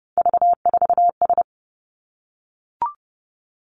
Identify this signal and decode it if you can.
Morse code, heard as a keyed tone, about 31 wpm, 700 Hz. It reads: V4S
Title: Three Challenging Letter-Number Combinations 30wpm